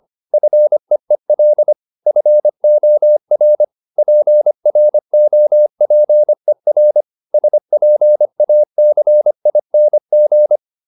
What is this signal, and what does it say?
Morse code, heard as a keyed tone, about 25 wpm, 600 Hz. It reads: FEEL FOR PROPER SPACING